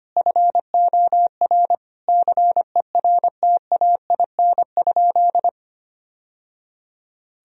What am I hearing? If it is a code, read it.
Morse code, heard as a keyed tone, about 25 wpm, 700 Hz. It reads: FOR CERTAIN?
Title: Sentences from Top 1000 Words - Review Part 2 - 25wpm